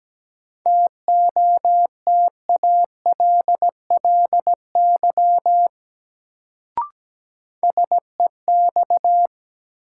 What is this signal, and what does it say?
Morse code, heard as a keyed tone, about 17 wpm, 700 Hz. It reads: TOTALLY SEX
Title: Sentences from Top 1600 Words - New Word Drill - 17wpm